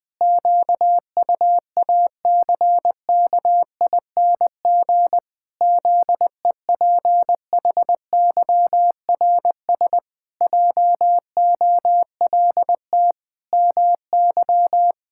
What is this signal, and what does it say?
Morse code, heard as a keyed tone, about 20 wpm, 700 Hz. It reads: QUACKING ZEPHYRS JOLT MY
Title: Pangram Challenge v2 - 20wpm